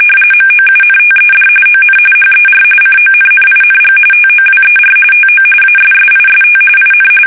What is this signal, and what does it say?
RTTY, about 45.45 baud. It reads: MOUNTED ON HER CONNING TOWER TO ASSIST IN
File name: fsk850 / USS-Massachusetts